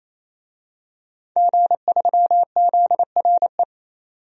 Morse code, heard as a keyed tone, about 28 wpm, 700 Hz. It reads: G3ZRE
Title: DXCC-100 Call Signs - 5 Characters - 28wpm